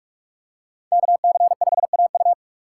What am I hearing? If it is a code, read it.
Morse code, heard as a keyed tone, about 46 wpm, 700 Hz. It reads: KC5AU
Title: USA Call Signs - 5 Characters - 6x Character Spacing - 45wpm